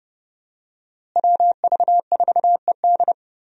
Morse code, heard as a keyed tone, about 30 wpm, 700 Hz. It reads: WV4ED
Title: USA Call Signs - 5 Characters - Repeated 2x - 30wpm